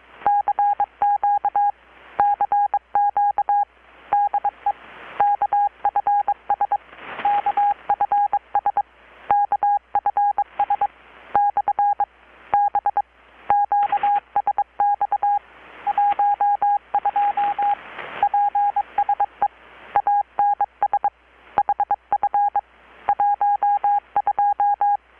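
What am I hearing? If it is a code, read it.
Morse code, heard as a keyed tone, about 22 wpm, 820 Hz. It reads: CQ CQ DE KFS KFS KFS/B QSX 12 PSE ANS HF 12
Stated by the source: Morse from KPH's Night of Nights XX maritime radiotelegraphy event, received in Chiapas, Mexico about 3,800 km away on a Kenwood R-600 with a grounded 10 m long-wire antenna.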